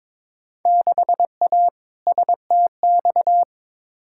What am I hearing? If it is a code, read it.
Morse code, heard as a keyed tone, about 22 wpm, 700 Hz. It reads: 6A STX
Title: Field Day Exchanges - 22wpm